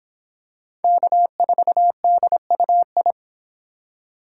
Morse code, heard as a keyed tone, about 24 wpm, 700 Hz. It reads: K4DUI